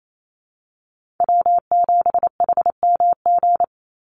Morse code, heard as a keyed tone, about 28 wpm, 700 Hz. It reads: W7HMG
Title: USA Call Signs 28wpm